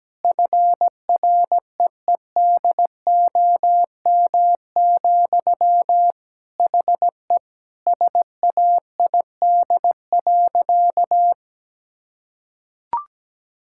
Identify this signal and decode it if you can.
Morse code, heard as a keyed tone, about 17 wpm, 700 Hz. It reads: FREEDOM, HE SAID.
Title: Sentences from Top 1000 Words - Review Part 5 - 17wpm